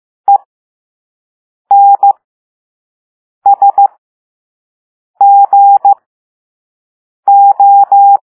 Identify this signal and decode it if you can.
Morse code, heard as a keyed tone, about 15 wpm, spaced as for 5 wpm, 800 Hz. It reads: ENSGO
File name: Morse / Level5-All-B